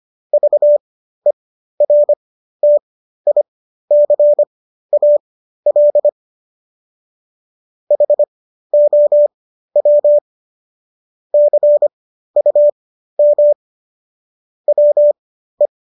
Morse code, heard as a keyed tone, about 25 wpm, spaced as for 13 wpm, 590 Hz. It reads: VERTICAL HOW CUM WE